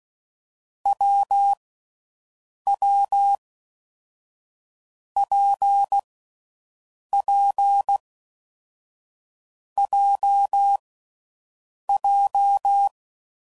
Morse code, heard as a keyed tone, about 16 wpm, 800 Hz. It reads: W W P P J J
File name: Combinations / Difficult Combos / WPJ WPJ1